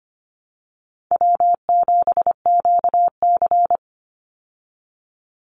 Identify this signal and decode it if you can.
Morse code, heard as a keyed tone, about 25 wpm, 700 Hz. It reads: W7QC
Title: Call Signs Repeated 3x - 25wpm